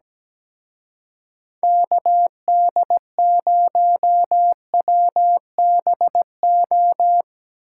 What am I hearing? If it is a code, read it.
Morse code, heard as a keyed tone, about 19 wpm, 700 Hz. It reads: KD0WBO
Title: Speed Racing - USA Call Signs 17wpm